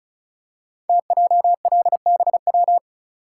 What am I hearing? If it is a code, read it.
Morse code, heard as a keyed tone, about 36 wpm, 700 Hz. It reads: TJLBW
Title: Code Groups - 5 Letter:Numbers - 35wpm